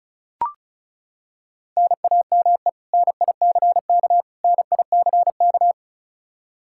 Morse code, heard as a keyed tone, about 36 wpm, 700 Hz. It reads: NAME NICK NICK
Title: Practice QSO - Exchange 1 - Beginning - 35wpm